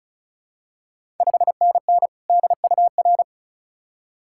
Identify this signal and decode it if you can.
Morse code, heard as a keyed tone, about 34 wpm, 700 Hz. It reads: 5NN DUR
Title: Ontario QSO Party Exchange 35wpm